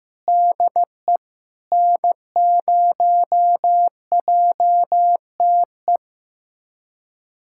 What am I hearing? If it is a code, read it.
Morse code, heard as a keyed tone, about 15 wpm, 700 Hz. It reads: DE N0JTE
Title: USA Callsign DE Callsign - 5 Characters - 15wpm